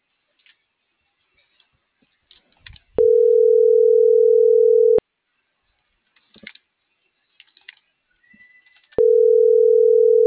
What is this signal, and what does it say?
Ringback tone (North America), 440 + 480 Hz.